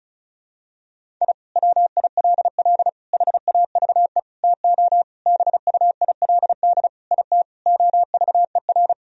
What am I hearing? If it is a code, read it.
Morse code, heard as a keyed tone, about 34 wpm, 700 Hz. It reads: I WILL HAVE TO BUILD IT OVER
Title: Sentences from Top 1000 Words - Review Part 1 - 35wpm